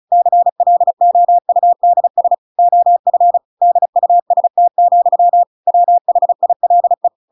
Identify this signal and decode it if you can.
Morse code, heard as a keyed tone, about 34 wpm, 700 Hz. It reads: CLOUDS OF DUST, WHILE
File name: Pinocchio / Pinocchio ch6 35 35 ik1hge